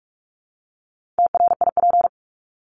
Morse code, heard as a keyed tone, about 44 wpm, 700 Hz. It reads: TRIP
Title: Top 1251-1500 Words 45wpm